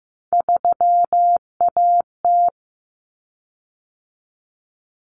Morse code, heard as a keyed tone, about 15 wpm, 700 Hz. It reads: 3AT
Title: Common Cut Numbers - 3-Digit 15wpm